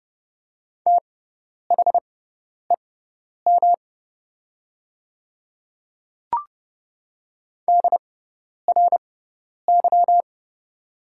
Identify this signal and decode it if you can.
Morse code, heard as a keyed tone, about 30 wpm, spaced as for 10 wpm, 700 Hz. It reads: THEM DRY
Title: 6x Character Spacing - Top 500 Words - 30wpm